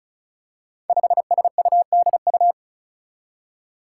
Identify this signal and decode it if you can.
Morse code, heard as a keyed tone, about 36 wpm, 700 Hz. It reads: 5SUDU
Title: Five Challenging Letter-Number Combinations 35wpm